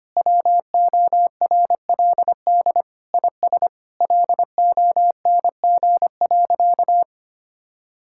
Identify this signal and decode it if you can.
Morse code, heard as a keyed tone, about 25 wpm, 700 Hz. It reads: WORLD IS LONG.
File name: practice / 25wpm / Sentences with 3-Word Phrases - 25wpm